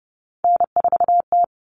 Morse code, heard as a keyed tone, about 30 wpm, 700 Hz. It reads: N4T
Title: All Cut Numbers - 3-Digit 30wpm